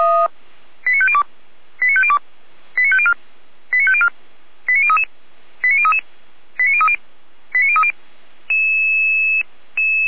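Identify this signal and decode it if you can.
Five-tone selective call (ZVEI), heard as quick runs of five tones, tones of about 70 ms. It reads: 89502 89502 89504 89504 89033 89033 89030 89030
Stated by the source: ZVEI five-tone call 89010 received on an SDR, archived by feuerwehr-orlamuende.org fire brigade.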